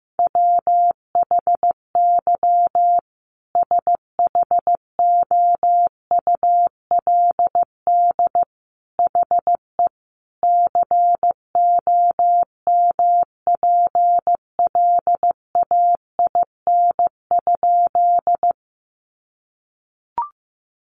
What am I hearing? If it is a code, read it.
Morse code, heard as a keyed tone, about 15 wpm, 700 Hz. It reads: WHY SHOULD HE COMPLAIN?
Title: Sentences from Top 1300 Words 15wpm